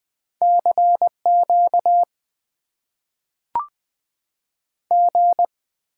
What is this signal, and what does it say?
Morse code, heard as a keyed tone, about 20 wpm, 700 Hz. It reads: CQ G
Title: Top 90 QSO Elements - 6x Character Spacing - 20wpm